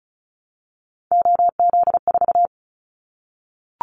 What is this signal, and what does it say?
Morse code, heard as a keyed tone, about 34 wpm, 700 Hz. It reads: OZ4
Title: Callsign Prefixes for Top 100 DXCC - 35wpm